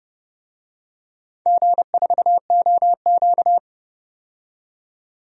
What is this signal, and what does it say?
Morse code, heard as a keyed tone, about 32 wpm, 700 Hz. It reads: G4OQ
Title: DXCC-100 Call Signs - 4 Characters - 30wpm